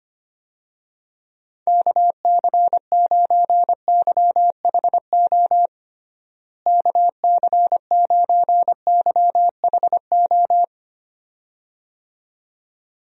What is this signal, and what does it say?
Morse code, heard as a keyed tone, about 25 wpm, 700 Hz. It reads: KC9YHO KC9YHO
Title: USA Call Signs - 6 Characters - Repeated 2x - 25wpm